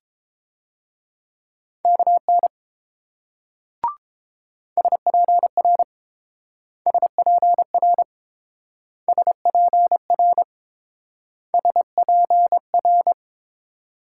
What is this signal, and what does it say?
Morse code, heard as a keyed tone, about 28 wpm, 700 Hz. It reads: KN SPR SPR SPR SPR